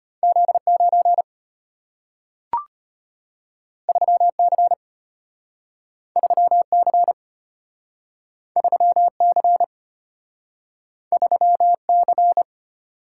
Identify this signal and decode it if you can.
Morse code, heard as a keyed tone, about 33 wpm, 690 Hz. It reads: Z9 3C 3C 3C 3C